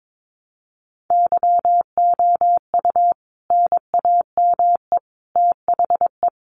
Morse code, heard as a keyed tone, about 22 wpm, 700 Hz. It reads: YOU NAME THE
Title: A Sound Of Thunder 22wpm